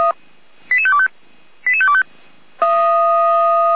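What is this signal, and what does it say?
Five-tone selective call (ZVEI), heard as quick runs of five tones, tones of about 70 ms. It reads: 88525 88525
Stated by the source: ZVEI five-tone call 88402 received by SDR, archived by Feuerwehr Orlamünde.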